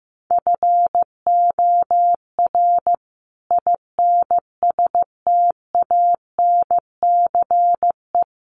Morse code, heard as a keyed tone, about 15 wpm, 700 Hz. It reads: FOR INSTANCE
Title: Sentences from Top 800 Words 15wpm